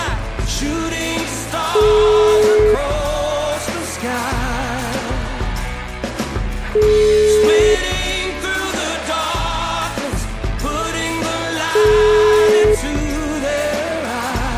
Ringback tone (Europe, 425 Hz), with noise.